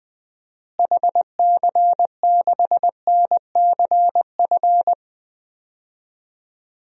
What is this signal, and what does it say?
Morse code, heard as a keyed tone, about 20 wpm, 700 Hz. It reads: HC6NCF
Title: DXCC-100 Call Signs - 6 Characters - Repeated 3x - 20wpm